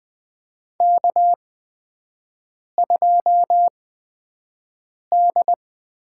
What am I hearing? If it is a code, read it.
Morse code, heard as a keyed tone, about 20 wpm, spaced as for 5 wpm, 700 Hz. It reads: K2D
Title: USA Call Signs - 4 Characters - 8x Character Spacing - 20wpm